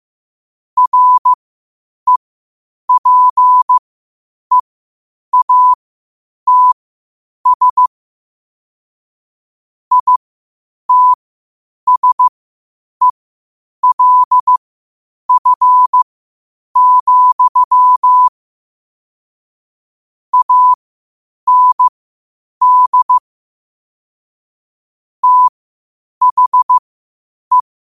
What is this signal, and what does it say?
Morse code, heard as a keyed tone, about 15 wpm, spaced as for 8 wpm, 1000 Hz. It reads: REPEATS ITSELF, AND THE
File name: qotdfiles / 8wpm